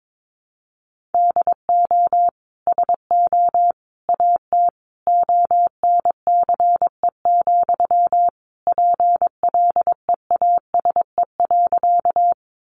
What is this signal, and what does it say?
Morse code, heard as a keyed tone, about 22 wpm, 700 Hz. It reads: DO SO AT ONCE, PLEASE.
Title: Sentences from Top 1500 Words 22wpm